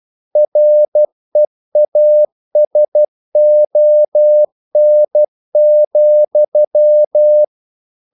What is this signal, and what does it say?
Morse code, heard as a keyed tone, about 12 wpm, 600 Hz. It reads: REASON,
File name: CW / Practice / EnglishQuotations0112